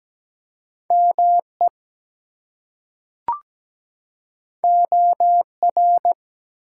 Morse code, heard as a keyed tone, about 17 wpm, 700 Hz. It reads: ME OR
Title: Two Letter Words - 17wpm